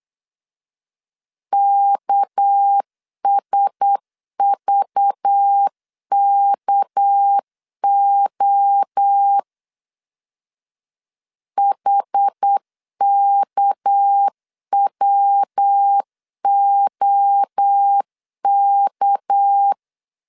Morse code, heard as a keyed tone, about 8 wpm, 800 Hz. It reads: KSVKO HKWOK
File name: LEZIONI / 6 PKX0 / 6 PK 8wpm 3003